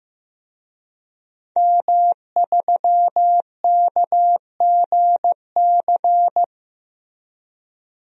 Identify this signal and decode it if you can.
Morse code, heard as a keyed tone, about 15 wpm, 700 Hz. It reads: M3KGC
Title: DXCC-100 Call Signs - 5 Characters - 15wpm